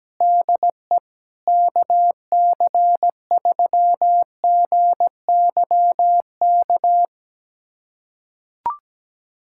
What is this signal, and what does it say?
Morse code, heard as a keyed tone, about 17 wpm, 700 Hz. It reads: DE KC3GYK
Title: USA Callsign DE Callsign 17wpm